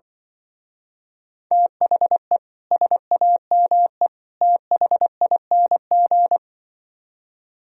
Morse code, heard as a keyed tone, about 24 wpm, 700 Hz. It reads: THE SAME THING
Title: Speed-Racing - Sets of 3 Words 20wpm